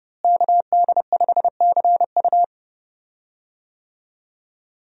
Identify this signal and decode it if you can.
Morse code, heard as a keyed tone, about 31 wpm, 700 Hz. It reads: KD5CU
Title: USA Call Signs - 5 Characters - Repeated 3x - 30wpm